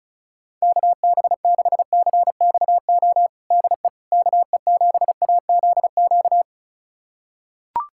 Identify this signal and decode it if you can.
Morse code, heard as a keyed tone, about 36 wpm, 700 Hz. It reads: KB6CXO DE KE7AZQ
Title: USA Callsign DE Callsign - 6 Characters - 35wpm